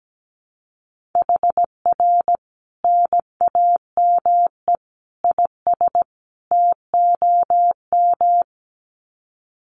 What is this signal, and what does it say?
Morse code, heard as a keyed tone, about 17 wpm, 700 Hz. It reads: HR NAME IS TOM
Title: Common QSO Phrases 17wpm